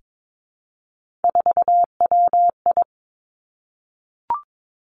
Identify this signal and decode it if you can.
Morse code, heard as a keyed tone, about 22 wpm, 700 Hz. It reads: 4WI
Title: Three Challenging Letter-Number Combinations 22wpm